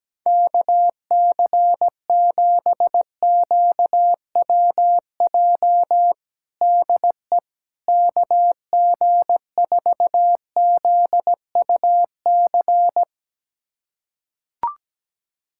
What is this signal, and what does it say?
Morse code, heard as a keyed tone, about 17 wpm, 700 Hz. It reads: KC7QWJ DE KG4ZUC
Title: USA Callsign DE Callsign - 6 Characters - 17wpm